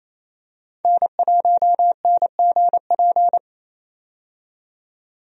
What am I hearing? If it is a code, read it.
Morse code, heard as a keyed tone, about 28 wpm, 700 Hz. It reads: N1NGP
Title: USA Call Signs - 5 Characters - 28wpm